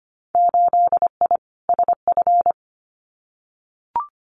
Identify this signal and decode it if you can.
Morse code, heard as a keyed tone, about 25 wpm, 700 Hz. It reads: 8I SF
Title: Winter Field Day Exchanges Encore 25wpm